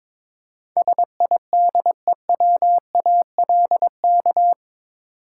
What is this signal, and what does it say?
Morse code, heard as a keyed tone, about 21 wpm, 700 Hz. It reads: SIDEWALK
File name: practice / 22wpm / Eight Letter Words - 22wpm